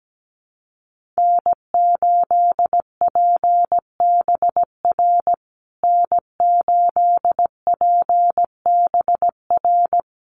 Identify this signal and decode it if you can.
Morse code, heard as a keyed tone, about 17 wpm, 700 Hz. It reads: N8PBR N8PBR
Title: USA Call Signs Repeated 2x 17wpm